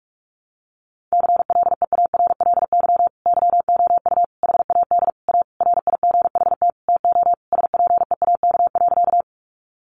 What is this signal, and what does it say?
Morse code, heard as a keyed tone, about 46 wpm, 700 Hz. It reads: CLEARLY YOU HAD A RIGHT TO SPEAK.
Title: Sentences from Top 1300 Words 45wpm